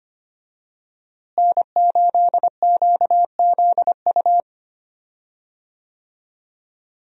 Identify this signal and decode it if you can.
Morse code, heard as a keyed tone, about 25 wpm, 700 Hz. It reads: N8QZU
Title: USA Call Signs - 5 Characters - Repeated 3x - 25wpm